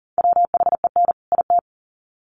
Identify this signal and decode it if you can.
Morse code, heard as a keyed tone, about 40 wpm, 700 Hz. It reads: WHEN IT
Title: Sets of 2 Words 40wpm